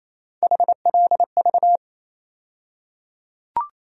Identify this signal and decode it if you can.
Morse code, heard as a keyed tone, about 28 wpm, 700 Hz. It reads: HLV
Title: USA Call Sign Postfixes - 3 Letters 28wpm